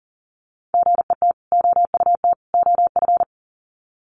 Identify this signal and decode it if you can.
Morse code, heard as a keyed tone, about 40 wpm, 700 Hz. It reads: GET OUT OF
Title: Sets of 3 Words 40wpm